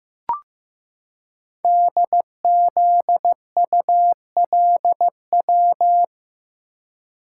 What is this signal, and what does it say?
Morse code, heard as a keyed tone, about 15 wpm, 700 Hz. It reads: DZULW
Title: Code Groups - 5 Letters - 15wpm